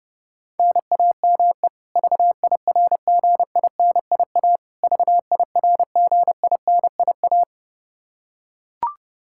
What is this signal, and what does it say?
Morse code, heard as a keyed tone, about 30 wpm, 700 Hz. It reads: NAME VIRGINIA VIRGINIA
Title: Practice QSO - Exchange 1 - Beginning - 30wpm